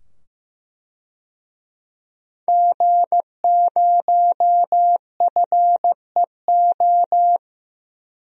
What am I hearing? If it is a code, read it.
Morse code, heard as a keyed tone, about 15 wpm, 700 Hz. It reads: G0FEO
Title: DXCC-100 Call Signs - 5 Characters - 15wpm